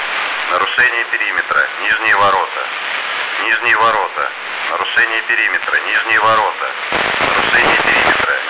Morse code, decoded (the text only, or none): none